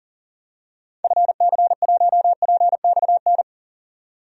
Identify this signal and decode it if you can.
Morse code, heard as a keyed tone, about 37 wpm, 700 Hz. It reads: FC1PXN
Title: DXCC-100 Call Signs - 6 Characters - 40wpm